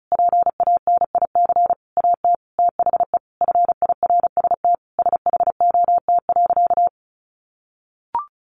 Morse code, heard as a keyed tone, about 34 wpm, 700 Hz. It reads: PANIC AT THE FIRST SHOT.
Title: A Sound Of Thunder 35wpm